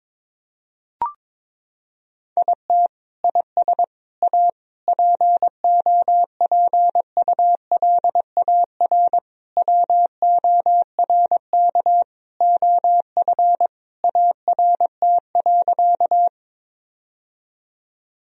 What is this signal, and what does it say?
Morse code, heard as a keyed tone, about 22 wpm, 700 Hz. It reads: IT IS A POPULAR WORK OF ART.
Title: Sentences from Top 1000 Words - Review Part 3 - 22wpm